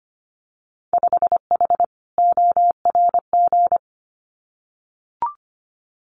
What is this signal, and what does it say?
Morse code, heard as a keyed tone, about 25 wpm, 700 Hz. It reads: 5H ORG
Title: Winter Field Day Exchanges Encore 25wpm